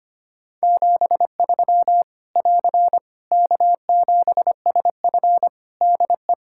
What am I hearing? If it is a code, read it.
Morse code, heard as a keyed tone, about 25 wpm, 700 Hz. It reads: 73 <AR> K7SF DE
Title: Practice QSO - Exchange 4 - Ending - 25wpm